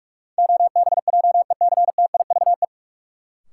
Morse code, heard as a keyed tone, about 44 wpm, 700 Hz. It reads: OBJECTIVE
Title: Sentences from Top 1200 Words - New Word Drill Alt - 45wpm